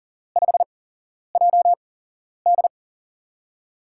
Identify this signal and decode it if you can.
Morse code, heard as a keyed tone, about 38 wpm, spaced as for 11 wpm, 700 Hz. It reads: 5JD
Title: USA Call Signs - 5 Characters - 8x Character Spacing - 40wpm